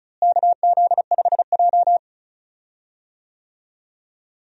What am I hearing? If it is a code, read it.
Morse code, heard as a keyed tone, about 36 wpm, 700 Hz. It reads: KZ5J
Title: USA Call Signs - 4 Characters - Repeated 2x - 35wpm